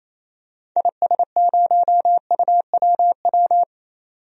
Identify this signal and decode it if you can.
Morse code, heard as a keyed tone, about 27 wpm, 700 Hz. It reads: IS0UWW
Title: DXCC-100 Call Signs - 6 Characters - 28wpm